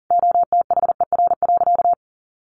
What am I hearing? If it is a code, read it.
Morse code, heard as a keyed tone, about 40 wpm, 700 Hz. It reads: OTHER.
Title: Easy Phrases 40wpm